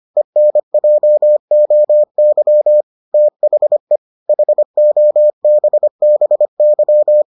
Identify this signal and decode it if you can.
Morse code, heard as a keyed tone, about 25 wpm, 600 Hz. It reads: ENJOY THE HOBBY